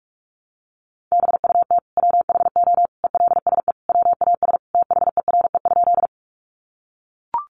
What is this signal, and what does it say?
Morse code, heard as a keyed tone, about 44 wpm, 700 Hz. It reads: BUT WHO ELSE WAS THERE?
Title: Sentences from Top 800 Words 45wpm